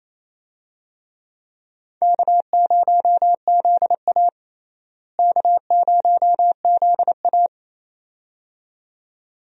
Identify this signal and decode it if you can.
Morse code, heard as a keyed tone, about 27 wpm, 700 Hz. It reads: K0ZA K0ZA
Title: USA Call Signs - 4 Characters - Repeated 2x - 28wpm